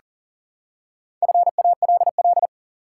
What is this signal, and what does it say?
Morse code, heard as a keyed tone, about 40 wpm, 700 Hz. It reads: FALL
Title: Four Letter Words - 40wpm